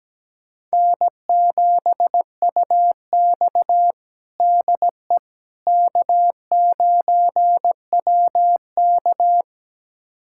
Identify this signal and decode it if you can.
Morse code, heard as a keyed tone, about 17 wpm, 700 Hz. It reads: N7UX DE K9WK
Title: USA Callsign DE Callsign - 4 Characters - 17wpm